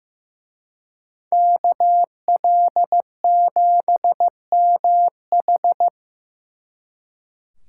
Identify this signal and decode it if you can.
Morse code, heard as a keyed tone, about 18 wpm, 700 Hz. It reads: KL7MH
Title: Speed Racing - USA 2x2 Call Signs 15wpm